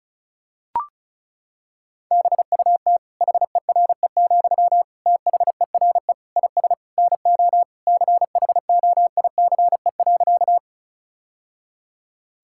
Morse code, heard as a keyed tone, about 34 wpm, 700 Hz. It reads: BUT HERE, THERE IS NO CHOICE.
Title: Sentences from Top 1000 Words - Review Part 6 - 35wpm